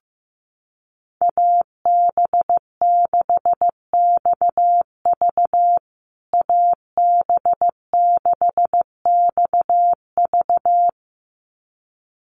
Morse code, heard as a keyed tone, about 15 wpm, 700 Hz. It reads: AB6XV AB6XV
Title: USA Call Signs Repeated 2x 15wpm